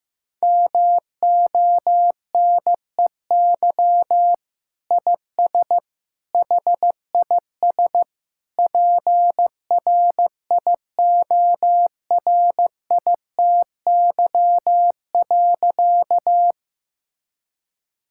Morse code, heard as a keyed tone, about 15 wpm, 700 Hz. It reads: MONEY IS HIS PRIORITY.
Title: Sentences from Top 1600 Words 15wpm